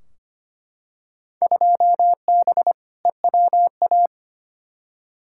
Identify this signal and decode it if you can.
Morse code, heard as a keyed tone, about 25 wpm, 700 Hz. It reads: 2B EWA
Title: Field Day Exchanges Encore - 25wpm